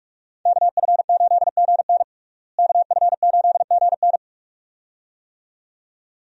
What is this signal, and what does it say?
Morse code, heard as a keyed tone, about 46 wpm, 700 Hz. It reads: KF8GN KF8GN
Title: USA Call Signs - 5 Characters - Repeated 2x - 45wpm